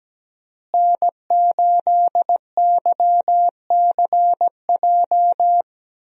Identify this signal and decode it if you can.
Morse code, heard as a keyed tone, about 17 wpm, 700 Hz. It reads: N8YCJ
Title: USA Call Signs - 5 Characters - Repeated 3x - 17wpm